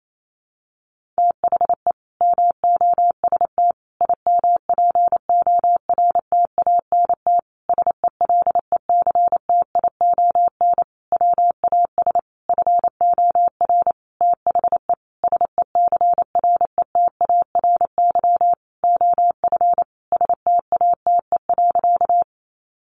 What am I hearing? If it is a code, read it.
Morse code, heard as a keyed tone, about 28 wpm, 700 Hz. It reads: THE MOST IMPORTANT SELECTION WAS FOR THE SECRETARY OF STATE.
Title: Sentences from Top 1400 Words 28wpm